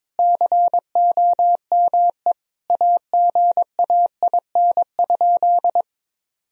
Morse code, heard as a keyed tone, about 21 wpm, 700 Hz. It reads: COME AGAIN?
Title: Sentences from Top 1000 Words - Review Part 4 - 22wpm